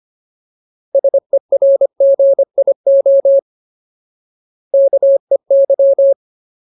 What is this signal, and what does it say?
Morse code, heard as a keyed tone, about 25 wpm, 550 Hz. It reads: SERGIO KEY
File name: Ultimate Word List / 25WPM Ultimate